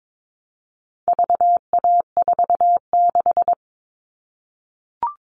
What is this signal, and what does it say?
Morse code, heard as a keyed tone, about 22 wpm, 700 Hz. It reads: VA46
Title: All Cut Numbers - 4-Digit 22wpm